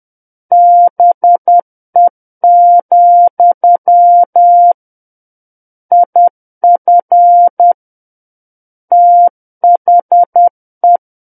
Morse code, heard as a keyed tone, about 10 wpm, 700 Hz. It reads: BE, IF THE
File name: Book One / 01-Chapter 1 - 10 WPM